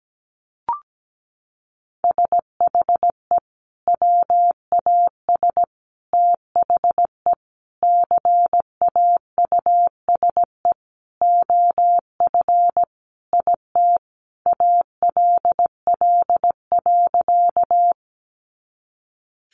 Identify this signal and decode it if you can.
Morse code, heard as a keyed tone, about 17 wpm, 700 Hz. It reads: SHE WAS THE CAUSE OF IT ALL.
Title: Sentences from Top 200 Words 17wpm